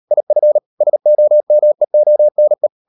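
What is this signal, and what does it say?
Morse code, heard as a keyed tone, about 38 wpm, 600 Hz. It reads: IF SOMEONE
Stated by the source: Morse code practice at 38 WPM, sending Ken Follett's novel Never in English, from CW med Gnister.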